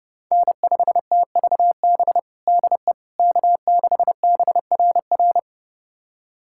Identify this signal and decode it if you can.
Morse code, heard as a keyed tone, about 30 wpm, 700 Hz. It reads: N5TVB DE K6BRR
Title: USA Callsign DE Callsign - 5 Characters - 30wpm